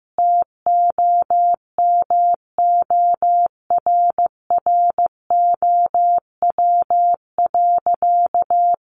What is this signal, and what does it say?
Morse code, heard as a keyed tone, about 15 wpm, 700 Hz. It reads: TOMORROW.
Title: Additional Easy Phrases 15wpm